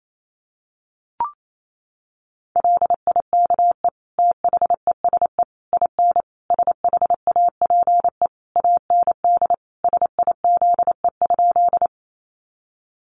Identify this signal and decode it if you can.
Morse code, heard as a keyed tone, about 28 wpm, 700 Hz. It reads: LIKE THESE IN SHAPE AND SIZE?
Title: Sentences from Top 500 Words 28wpm